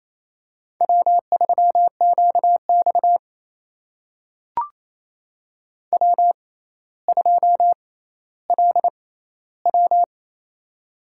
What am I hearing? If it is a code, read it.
Morse code, heard as a keyed tone, about 28 wpm, spaced as for 14 wpm, 700 Hz. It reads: W3QX W2LW
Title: USA Call Signs - 4 Characters - 6x Character Spacing - 28wpm